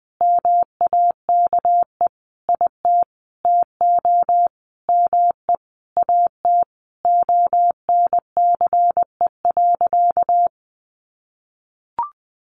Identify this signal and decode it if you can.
Morse code, heard as a keyed tone, about 20 wpm, 700 Hz. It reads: MAKE IT TO ME AT ONCE.
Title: Sentences from Top 500 Words 20wpm